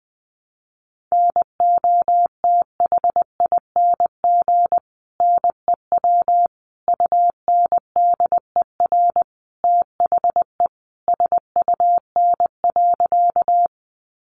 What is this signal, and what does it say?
Morse code, heard as a keyed tone, about 20 wpm, 700 Hz. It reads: NOTHING NEW UNDER THE SUN.
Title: Sentences from Top 500 Words 20wpm